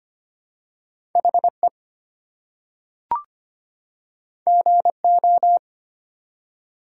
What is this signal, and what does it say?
Morse code, heard as a keyed tone, about 25 wpm, 700 Hz. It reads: HE GO
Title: Two Letter Words - 25wpm